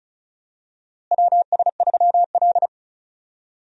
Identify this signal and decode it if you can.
Morse code, heard as a keyed tone, about 36 wpm, 700 Hz. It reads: WS3L
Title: USA Call Signs - 4 Characters - Repeated 2x - 35wpm